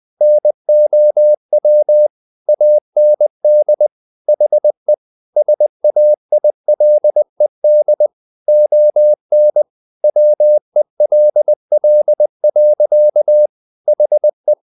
Morse code, heard as a keyed tone, about 20 wpm, 600 Hz. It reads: NOW AND HE SAILED ON WELL. HE
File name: the old man and the sea part 2 20wpm